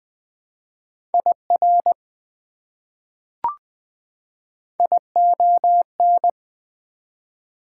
Morse code, heard as a keyed tone, about 20 wpm, 700 Hz. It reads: IR ION
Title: Prefixes and Suffixes 20wpm